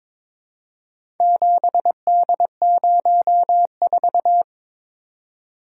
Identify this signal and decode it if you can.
Morse code, heard as a keyed tone, about 22 wpm, 700 Hz. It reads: 7D04
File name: practice / 22wpm / Four Letter-Number Combinations - 22wpm